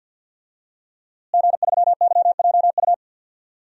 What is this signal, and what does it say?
Morse code, heard as a keyed tone, about 50 wpm, 700 Hz. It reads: G3YJU
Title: DXCC-100 Call Signs - 5 Characters - Repeated 3x - 50wpm